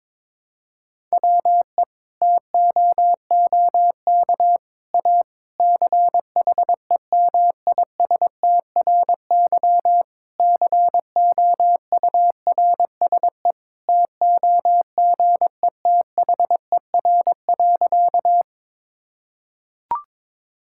Morse code, heard as a keyed tone, about 22 wpm, 700 Hz. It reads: WE TOOK A CHEMISTRY COURSE TOGETHER.
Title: Sentences from Top 1000 Words - Review Part 6 - 22wpm